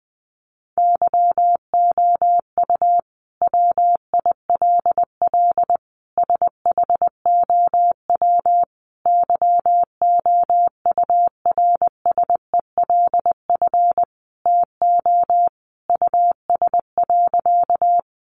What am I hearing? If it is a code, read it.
Morse code, heard as a keyed tone, about 19 wpm, 700 Hz. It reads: YOU WILL SHOW YOURSELF TO US.
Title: Sentences from Top 600 Words 20wpm